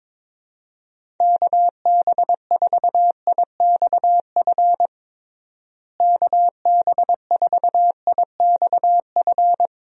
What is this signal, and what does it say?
Morse code, heard as a keyed tone, about 22 wpm, 700 Hz. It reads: KB4IXF KB4IXF
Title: USA Call Signs - 6 Characters - Repeated 3x - 22wpm